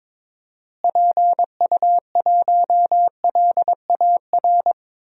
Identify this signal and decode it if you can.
Morse code, heard as a keyed tone, about 22 wpm, 700 Hz. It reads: PU1LAR
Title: DXCC-100 Call Signs - 6 Characters - Repeated 3x - 22wpm